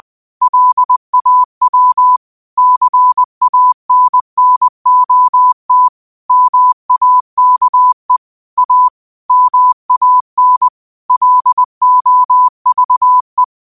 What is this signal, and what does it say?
Morse code, heard as a keyed tone, about 20 wpm, 1000 Hz. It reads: LAW CANNOT MAKE A MAN LOVE
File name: 20wpm